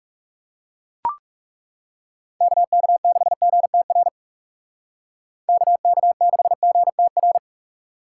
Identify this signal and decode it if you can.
Morse code, heard as a keyed tone, about 35 wpm, 700 Hz. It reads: KK6GTR KK6GTR
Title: Speed Racing - USA Call Signs 30wpm